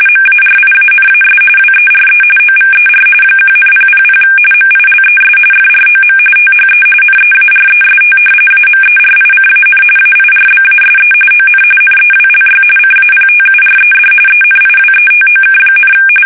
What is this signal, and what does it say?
RTTY, about 45.45 baud. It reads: MARIANAS WERE OVER 1,000 MILES FROM THE NEAREST AMERICAN AIRBASE ON NEWLY CAPTURED ENIWETOK,
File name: fsk850 / battleshipnj